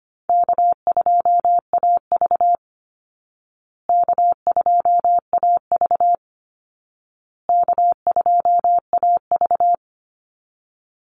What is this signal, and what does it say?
Morse code, heard as a keyed tone, about 25 wpm, 700 Hz. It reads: K2AV K2AV K2AV
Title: Easy Call Signs Repeated 3x - 25wpm